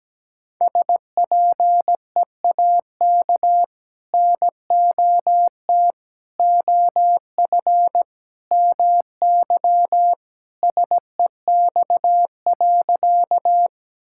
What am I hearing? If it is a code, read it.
Morse code, heard as a keyed tone, about 17 wpm, 700 Hz. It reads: SPEAK NOT OF MY SEX.
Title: Sentences from Top 1600 Words 17wpm